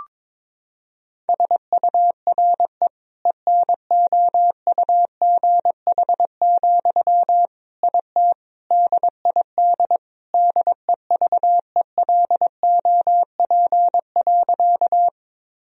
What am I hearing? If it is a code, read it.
Morse code, heard as a keyed tone, about 22 wpm, 700 Hz. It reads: SURE ENOUGH, IT DID DEVELOP.
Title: Sentences from Top 1000 Words - Review Part 2 - 22wpm